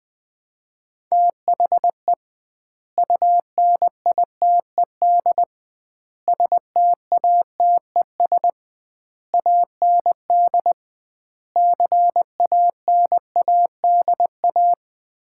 Morse code, heard as a keyed tone, about 20 wpm, 700 Hz. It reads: THE UNITED STATES AND CANADA
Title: 2x Word-Spacing - Sets of 5 Words - 20wpm